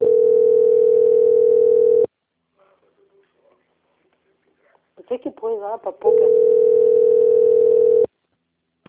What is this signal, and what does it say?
Ringback tone (North America), 440 + 480 Hz, with speech.
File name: out- - - - - .51765